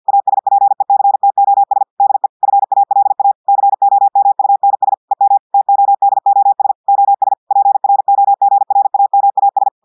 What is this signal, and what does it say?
Morse code, heard as a keyed tone, about 50 wpm, 800 Hz. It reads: ASPECTOS DE FALA COMUNS EM TODOS OS PROGRAMAS
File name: audios / portugais / portugais texte en clair 2 vitesse 50 mots minutes